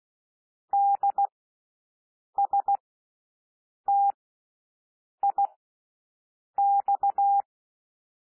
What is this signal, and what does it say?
Morse code, heard as a keyed tone, about 16 wpm, spaced as for 6 wpm, 800 Hz. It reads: DSTIX
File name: SOUND LETTERS 16 wpm